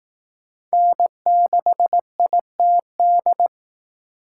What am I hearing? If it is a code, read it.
Morse code, heard as a keyed tone, about 20 wpm, 700 Hz. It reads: N6ITD